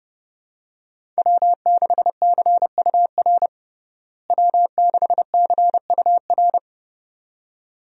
Morse code, heard as a keyed tone, about 32 wpm, 700 Hz. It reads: W6CUR W6CUR
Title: USA Call Signs - 5 Characters - Repeated 2x - 30wpm